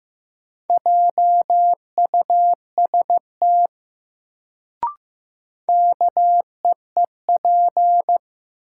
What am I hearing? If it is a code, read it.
Morse code, heard as a keyed tone, about 15 wpm, 700 Hz. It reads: JUST KEEP
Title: Sets of 2 Words - New Word Drill - 15wpm